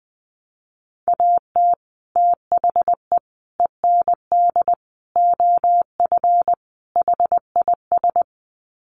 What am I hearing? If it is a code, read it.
Morse code, heard as a keyed tone, about 20 wpm, 700 Hz. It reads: AT THE END OF HIS
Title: Sets of 5 Words Encore 20wpm